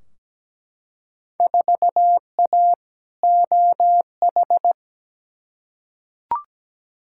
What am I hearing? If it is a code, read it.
Morse code, heard as a keyed tone, about 17 wpm, 700 Hz. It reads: 4A OH
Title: Field Day Exchanges - 17wpm